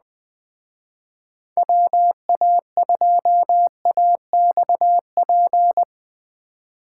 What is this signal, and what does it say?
Morse code, heard as a keyed tone, about 19 wpm, 700 Hz. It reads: WA2AXP
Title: Speed Racing - USA Call Signs 17wpm